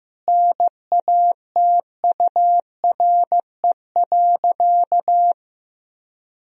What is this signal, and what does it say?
Morse code, heard as a keyed tone, about 15 wpm, 700 Hz. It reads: NATURE.
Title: Sentences from Top 1000 Words - Review Part 3 - 15wpm